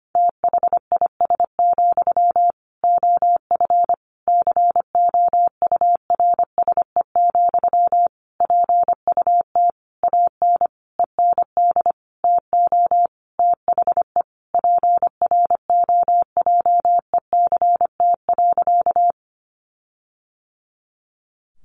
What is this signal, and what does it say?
Morse code, heard as a keyed tone, about 25 wpm, 700 Hz. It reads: THIS, OF COURSE, PUT AN END TO THE PROJECT.
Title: Sentences from Top 1400 Words 25wpm